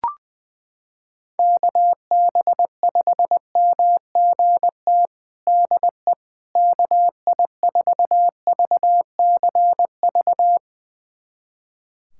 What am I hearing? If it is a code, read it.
Morse code, heard as a keyed tone, about 21 wpm, 700 Hz. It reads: KB5MGT DE KI4VCV
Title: USA Callsign DE Callsign - 6 Characters - 20wpm